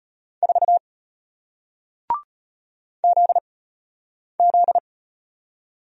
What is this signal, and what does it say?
Morse code, heard as a keyed tone, about 33 wpm, 700 Hz. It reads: 4 Z Z